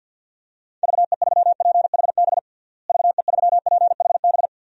Morse code, heard as a keyed tone, about 45 wpm, 700 Hz. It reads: VE3PHB VE3PHB
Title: DXCC-100 Call Signs - 6 Characters - Repeated 3x - 50wpm